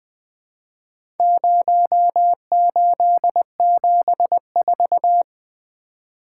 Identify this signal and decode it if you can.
Morse code, heard as a keyed tone, about 20 wpm, 700 Hz. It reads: 0874
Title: Four Letter-Number Combinations - 20wpm